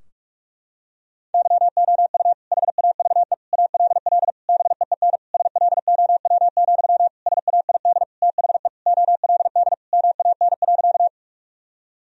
Morse code, heard as a keyed tone, about 46 wpm, 700 Hz. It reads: YOU HAVE ALL BEEN SLOW, SAID THE OLD MAN.
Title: Sentences from Top 1000 Words - Review Part 2 - 45wpm